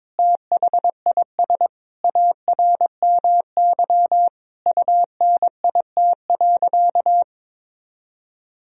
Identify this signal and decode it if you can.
Morse code, heard as a keyed tone, about 22 wpm, 700 Hz. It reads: THIS ARMY UNIT.
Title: Sentences from Top 1000 Words - Review Part 2 - 22wpm